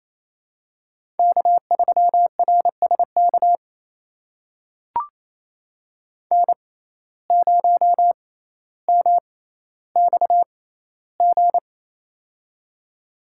Morse code, heard as a keyed tone, about 28 wpm, spaced as for 14 wpm, 700 Hz. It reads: K3RSK N0MXG
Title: USA Call Signs - 5 Characters - 6x Character Spacing - 28wpm